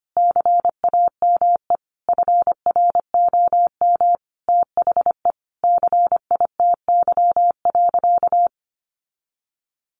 Morse code, heard as a keyed tone, about 25 wpm, 700 Hz. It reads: CAME FROM THE CITY.
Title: Sentences from Top 1500 Words 25wpm